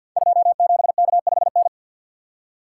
Morse code, heard as a keyed tone, about 47 wpm, 700 Hz. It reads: 27K5N
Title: Code Groups - 5 Letter:Numbers - 50wpm